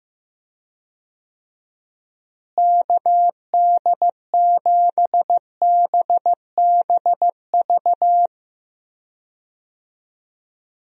Morse code, heard as a keyed tone, about 15 wpm, 700 Hz. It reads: KD7BBV
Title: USA Call Signs - 6 Characters - Repeated 2x - 15wpm